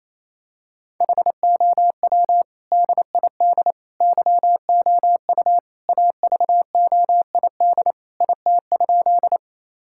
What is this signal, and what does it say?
Morse code, heard as a keyed tone, about 27 wpm, 700 Hz. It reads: HOW DID YOU AVOID IT?
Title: Sentences from Top 1000 Words - Review Part 6 - 28wpm